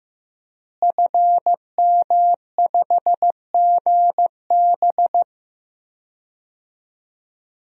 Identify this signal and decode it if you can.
Morse code, heard as a keyed tone, about 15 wpm, 700 Hz. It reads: FM5GB
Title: DXCC-100 Call Signs - 5 Characters - Repeated 3x - 15wpm